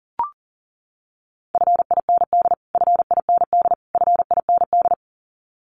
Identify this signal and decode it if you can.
Morse code, heard as a keyed tone, about 40 wpm, 700 Hz. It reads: FIND FIND FIND
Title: Top 300 Words Repeated 40wpm